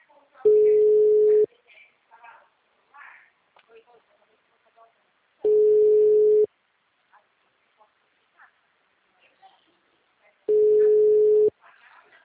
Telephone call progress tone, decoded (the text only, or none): ringback tone